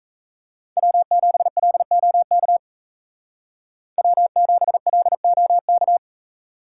Morse code, heard as a keyed tone, about 33 wpm, 700 Hz. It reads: W7LOK W7LOK
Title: Speed Racing - USA 1x3 Call Signs 28wpm